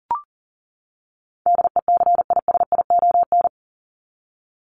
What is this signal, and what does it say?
Morse code, heard as a keyed tone, about 40 wpm, 700 Hz. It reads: DECISION
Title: Top 751-1000 Words 40wpm